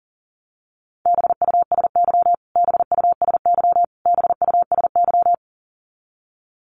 Morse code, heard as a keyed tone, about 40 wpm, 700 Hz. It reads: BUSY BUSY BUSY
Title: Top 500 Words Repeated 40wpm